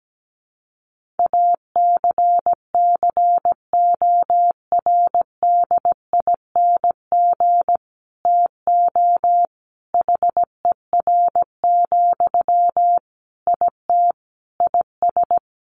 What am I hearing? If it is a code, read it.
Morse code, heard as a keyed tone, about 17 wpm, 700 Hz. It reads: ACCORDING TO HER, IT IS
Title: Sentences from Top 800 Words 17wpm